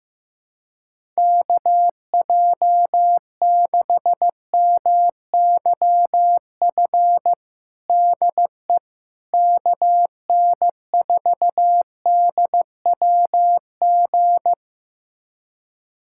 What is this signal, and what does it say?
Morse code, heard as a keyed tone, about 15 wpm, 700 Hz. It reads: KJ6MYF DE KN4DWG
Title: USA Callsign DE Callsign 15wpm